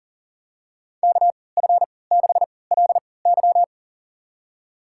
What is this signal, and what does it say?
Morse code, heard as a keyed tone, about 40 wpm, spaced as for 23 wpm, 700 Hz. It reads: KF6LY
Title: USA Call Signs - 5 Characters - 3x Character Spacing - 40wpm